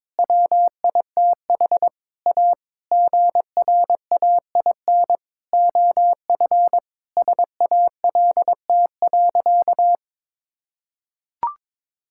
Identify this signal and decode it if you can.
Morse code, heard as a keyed tone, about 22 wpm, 700 Hz. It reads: WITH A GRAIN OF SALT.
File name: practice / 22wpm / Sentences with 8-Word Phrases - 22wpm